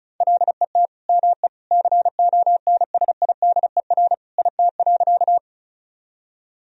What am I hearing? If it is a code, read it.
Morse code, heard as a keyed tone, about 36 wpm, 700 Hz. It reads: LET ME CONSIDER IT.
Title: Sentences from Top 1000 Words - Review Part 4 - 35wpm